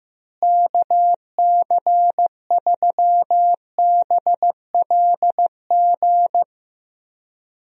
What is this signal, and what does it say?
Morse code, heard as a keyed tone, about 15 wpm, 700 Hz. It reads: KC3BLG
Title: USA Call Signs - 6 Characters - Repeated 3x - 15wpm